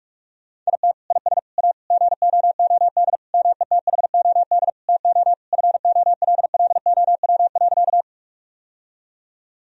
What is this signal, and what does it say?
Morse code, heard as a keyed tone, about 46 wpm, 700 Hz. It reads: IT IS A GOOD METHOD TO FOLLOW.
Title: Sentences from Top 1000 Words - Review Part 3 - 45wpm